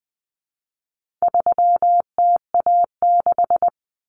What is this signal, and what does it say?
Morse code, heard as a keyed tone, about 20 wpm, 700 Hz. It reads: 3TA6
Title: Common Cut Numbers - 4-Digit 20wpm